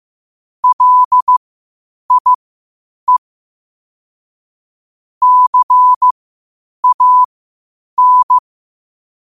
Morse code, heard as a keyed tone, about 15 wpm, spaced as for 8 wpm, 1000 Hz. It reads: LIE CAN